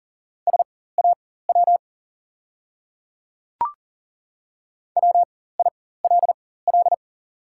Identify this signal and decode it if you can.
Morse code, heard as a keyed tone, about 40 wpm, spaced as for 19 wpm, 700 Hz. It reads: SAW WILL
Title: 4x Character Spacing - Top 500 Words - 40wpm